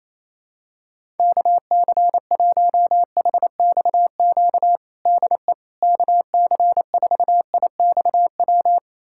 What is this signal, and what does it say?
Morse code, heard as a keyed tone, about 28 wpm, 700 Hz. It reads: KC1HXQ DE KC4IXW
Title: USA Callsign DE Callsign - 6 Characters - 28wpm